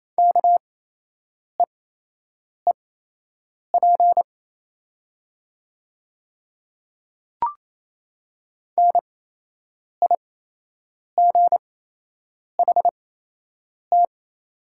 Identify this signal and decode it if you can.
Morse code, heard as a keyed tone, about 28 wpm, spaced as for 8 wpm, 700 Hz. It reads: KEEP NIGHT
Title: 8x Character Spacing - Top 300 Words - 28wpm